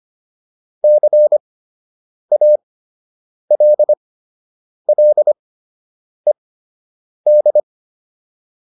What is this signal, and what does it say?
Morse code, heard as a keyed tone, about 25 wpm, spaced as for 8 wpm, 600 Hz. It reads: CALLED